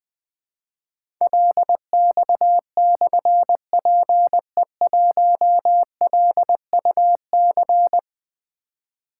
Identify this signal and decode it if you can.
Morse code, heard as a keyed tone, about 20 wpm, 700 Hz. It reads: LX/PE1LUC
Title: Difficult Call Signs 20wpm